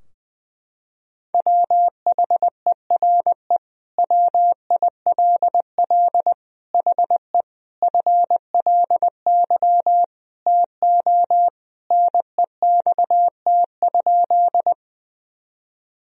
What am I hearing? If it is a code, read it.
Morse code, heard as a keyed tone, about 20 wpm, 700 Hz. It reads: WHERE WILL HE FLY TO NEXT?
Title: Sentences from Top 1000 Words - Review Part 2 - 20wpm